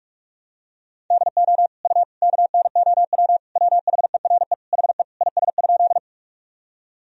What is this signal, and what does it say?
Morse code, heard as a keyed tone, about 41 wpm, 700 Hz. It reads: DO U KNOW WHERE HE IS?
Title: Sentences from Top 300 Words and Abbreviations 45wpm